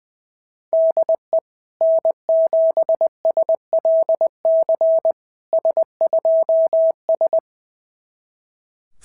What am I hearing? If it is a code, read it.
Morse code, heard as a keyed tone, about 20 wpm, 690 Hz. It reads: DE N7SLC S2S
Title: SOTA QSO Practice 20wpm